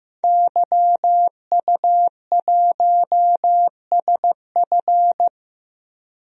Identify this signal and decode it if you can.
Morse code, heard as a keyed tone, about 15 wpm, 700 Hz. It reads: YU1SF
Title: DXCC-100 Call Signs - 5 Characters - Repeated 2x - 15wpm